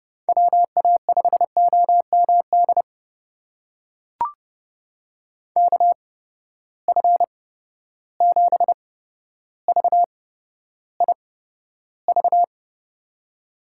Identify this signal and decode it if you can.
Morse code, heard as a keyed tone, about 30 wpm, spaced as for 13 wpm, 700 Hz. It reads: WA5OMD KF7VIV
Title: USA Call Signs - 6 Characters - 8x Character Spacing - 30wpm